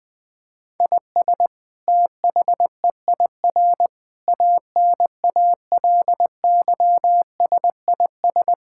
Morse code, heard as a keyed tone, about 20 wpm, 700 Hz. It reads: IS THEIR ANALYSIS
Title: Sentences from Top 1000 Words - Review Part 5 - 20wpm